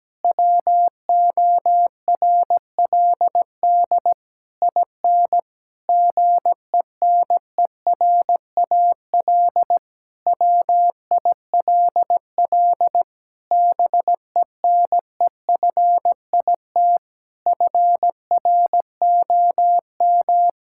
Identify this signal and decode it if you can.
Morse code, heard as a keyed tone, about 17 wpm, 700 Hz. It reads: WORLD IN GENERAL WILL BENEFIT FROM
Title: Sentences from Top 1000 Words - Review Part 6 - 17wpm